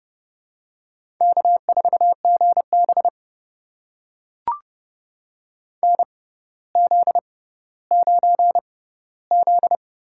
Morse code, heard as a keyed tone, about 30 wpm, spaced as for 15 wpm, 700 Hz. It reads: K4GB NZ9Z
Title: USA Call Signs - 4 Characters - 6x Character Spacing - 30wpm